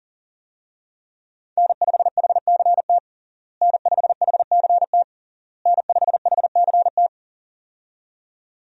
Morse code, heard as a keyed tone, about 40 wpm, 700 Hz. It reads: N5HCT N5HCT N5HCT
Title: USA Call Signs - 5 Characters - Repeated 3x - 40wpm